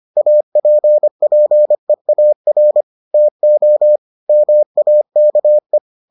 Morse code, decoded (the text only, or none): APPEAR TO MAKE